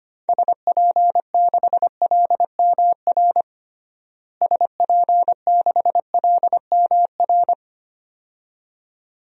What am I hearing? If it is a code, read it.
Morse code, heard as a keyed tone, about 25 wpm, 700 Hz. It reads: SP6LMR SP6LMR
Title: DXCC-100 Call Signs - 6 Characters - Repeated 2x - 25wpm